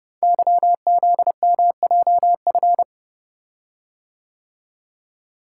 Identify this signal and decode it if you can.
Morse code, heard as a keyed tone, about 30 wpm, 700 Hz. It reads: YZMJF
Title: Code Groups - 5 Letters - 30wpm